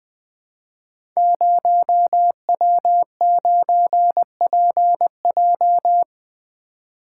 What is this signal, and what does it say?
Morse code, heard as a keyed tone, about 20 wpm, 700 Hz. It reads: 0W9PJ
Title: Code Groups - 5 Letter:Numbers - 20wpm